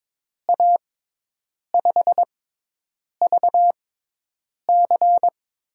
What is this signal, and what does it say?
Morse code, heard as a keyed tone, about 22 wpm, spaced as for 8 wpm, 700 Hz. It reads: A5VC
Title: USA Call Signs - 6 Characters - 6x Character Spacing - 22wpm